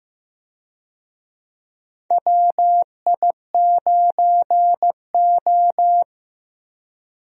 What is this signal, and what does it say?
Morse code, heard as a keyed tone, about 15 wpm, 700 Hz. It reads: WI9O
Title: USA Call Signs - 4 Characters - Repeated 2x - 15wpm